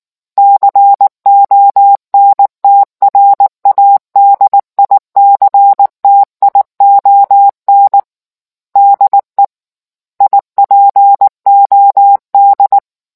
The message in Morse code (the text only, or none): CONTRADICTION DE IPOD